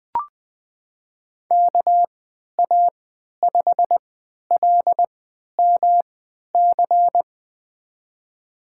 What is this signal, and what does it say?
Morse code, heard as a keyed tone, about 20 wpm, spaced as for 11 wpm, 700 Hz. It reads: KA5LMC
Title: USA Call Signs - 6 Characters - 3x Character Spacing - 20wpm